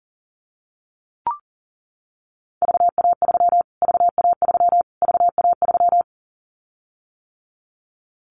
Morse code, heard as a keyed tone, about 40 wpm, 700 Hz. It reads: VA3 VA3 VA3
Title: Callsign Prefixes for Top 25 DXCC - Repeated 3x - 40wpm